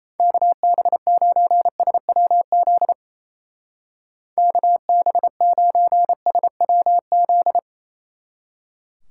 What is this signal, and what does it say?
Morse code, heard as a keyed tone, about 34 wpm, 700 Hz. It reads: KB9SWZ KB9SWZ
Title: Speed Racing - USA Call Signs 28wpm